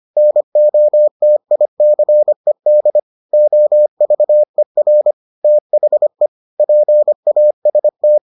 Morse code, heard as a keyed tone, about 25 wpm, 600 Hz. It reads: NOTICED OVER THE PAST